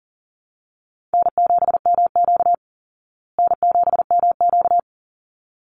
Morse code, heard as a keyed tone, about 40 wpm, 700 Hz. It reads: N7MQ N7MQ
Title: Easy Call Signs Repeated 2x - 40wpm